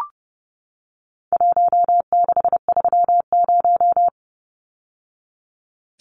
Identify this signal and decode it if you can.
Morse code, heard as a keyed tone, about 30 wpm, 700 Hz. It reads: 1630
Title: Numbers - Contest Style - 4-Digit 30wpm